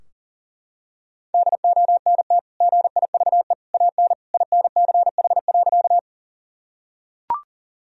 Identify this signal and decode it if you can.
Morse code, heard as a keyed tone, about 40 wpm, 700 Hz. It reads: DONT GIVE AN INCH.
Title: Sentences from Top 1000 Words - Review Part 3 - 40wpm